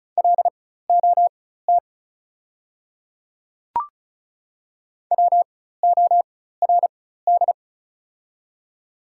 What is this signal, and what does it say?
Morse code, heard as a keyed tone, about 34 wpm, spaced as for 16 wpm, 690 Hz. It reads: LOT WORD
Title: 4x Character Spacing - Top 500 Words - 35wpm